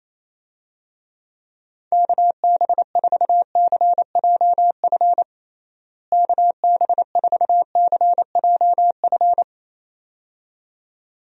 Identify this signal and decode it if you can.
Morse code, heard as a keyed tone, about 28 wpm, 700 Hz. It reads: KB4CJF KB4CJF
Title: USA Call Signs - 6 Characters - Repeated 2x - 28wpm